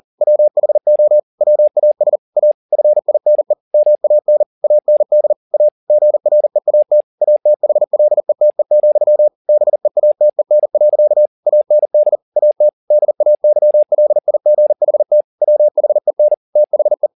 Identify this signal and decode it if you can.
Morse code, heard as a keyed tone, about 40 wpm, 600 Hz. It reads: WHO WAS A FINE MAN AND A GREAT ATHLETE, BEATEN. AND AT DAYLIGHT WHEN THE
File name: the old man and the sea 2 40wpm